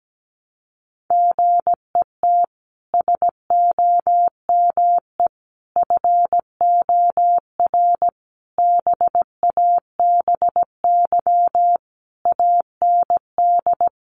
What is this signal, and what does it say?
Morse code, heard as a keyed tone, about 17 wpm, 700 Hz. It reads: GET SOME FOR BABY AND
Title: Sentences from Top 1300 Words 17wpm